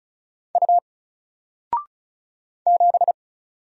Morse code, heard as a keyed tone, about 36 wpm, 700 Hz. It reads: U 7
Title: Single Letter-Number Alt 35wpm